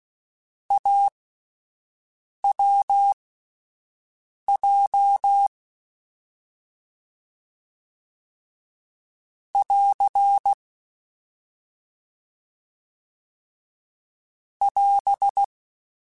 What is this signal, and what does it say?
Morse code, heard as a keyed tone, about 16 wpm, spaced as for 5 wpm, 800 Hz. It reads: AWJ <AR> <AS>
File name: Difficult Combos / Difficult Combos-mp3s / AWJ ShortRun @5